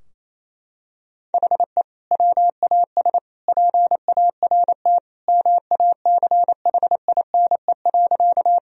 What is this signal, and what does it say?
Morse code, heard as a keyed tone, about 27 wpm, 700 Hz. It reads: HE WAS PART MACHINE.
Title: Sentences from Top 1000 Words - Review Part 2 - 28wpm